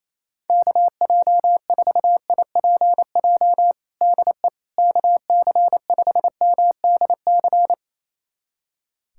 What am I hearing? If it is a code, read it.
Morse code, heard as a keyed tone, about 28 wpm, 700 Hz. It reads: KJ4IPJ DE KC5MDC
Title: USA Callsign DE Callsign - 6 Characters - 28wpm